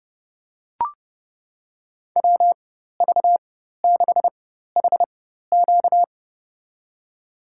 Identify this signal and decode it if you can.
Morse code, heard as a keyed tone, about 30 wpm, spaced as for 14 wpm, 700 Hz. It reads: WV6HQ
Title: USA Call Signs - 5 Characters - 4x Character Spacing - 30wpm